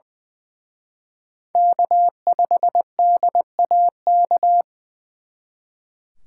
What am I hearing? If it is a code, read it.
Morse code, heard as a keyed tone, about 24 wpm, 700 Hz. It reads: K5DAK
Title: Speed Racing - USA 1x3 Call Signs 20wpm